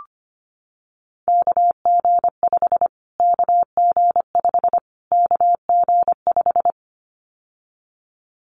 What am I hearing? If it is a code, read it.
Morse code, heard as a keyed tone, about 25 wpm, 700 Hz. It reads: KG5 KG5 KG5
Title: Callsign Prefixes for Top 25 DXCC - Repeated 3x - 25wpm